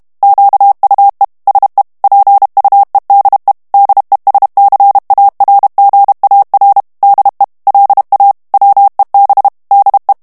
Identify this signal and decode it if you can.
Morse code, heard as a keyed tone, about 32 wpm, 800 Hz. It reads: QUE SE PUEDE DESCARGAR DE LA WEB DE
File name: morse mp3 / ESPAGNOL TEXTE EN CLAIR 1 VITESSE 30 MOTS MINUTES